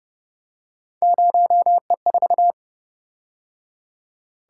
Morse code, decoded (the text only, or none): 0E4